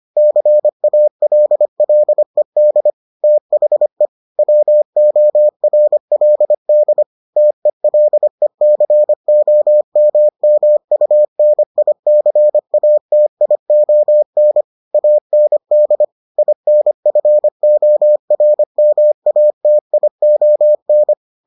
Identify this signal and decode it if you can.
Morse code, heard as a keyed tone, about 25 wpm, 600 Hz. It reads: CALLED THE WORLD TELECOMMUNICATION AND INFORMATION